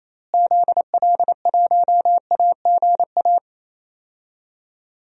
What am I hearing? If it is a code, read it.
Morse code, heard as a keyed tone, about 28 wpm, 700 Hz. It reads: ZL1AGA
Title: DXCC-100 Call Signs - 6 Characters - Repeated 3x - 28wpm